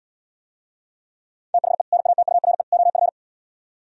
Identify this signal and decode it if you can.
Morse code, heard as a keyed tone, about 79 wpm, 700 Hz. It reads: THE NATURE OF